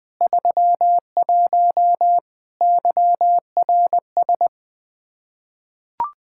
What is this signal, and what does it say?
Morse code, heard as a keyed tone, about 21 wpm, 700 Hz. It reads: 31 YRS
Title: K1USN SST QSO Practice 20wpm